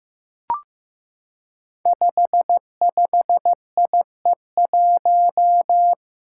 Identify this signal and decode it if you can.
Morse code, heard as a keyed tone, about 15 wpm, 700 Hz. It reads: 55IE1
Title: Five Challenging Letter-Number Combinations 15wpm